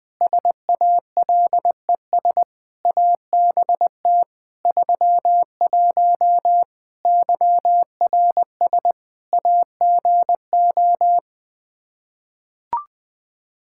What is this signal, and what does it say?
Morse code, heard as a keyed tone, about 21 wpm, 700 Hz. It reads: SALES ABT 31 YRS AGO
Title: Practice QSO - Exchange 3 - More Info - 20wpm